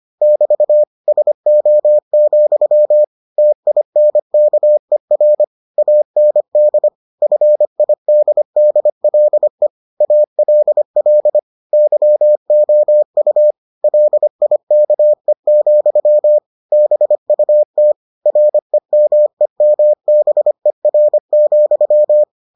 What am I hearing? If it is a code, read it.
Morse code, heard as a keyed tone, about 25 wpm, 600 Hz. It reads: <BT> SO, TINKER AND FIDDLE ALL YOU LIKE, BUT REMEMBER,